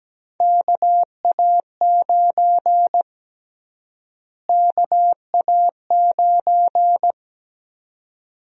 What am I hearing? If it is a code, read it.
Morse code, heard as a keyed tone, about 17 wpm, 700 Hz. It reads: KA9 KA9
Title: USA Call Sign Prefixes - Repeated 2x - 17wpm